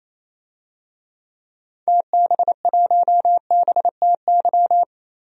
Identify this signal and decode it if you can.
Morse code, heard as a keyed tone, about 28 wpm, 700 Hz. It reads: TB1BTY
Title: DXCC-100 Call Signs - 6 Characters - 28wpm